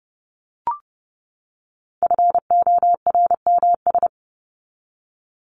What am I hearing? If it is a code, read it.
Morse code, heard as a keyed tone, about 30 wpm, 700 Hz. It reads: FORMS
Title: ES:S Suffix - Top 100 Words with Base Words 30wpm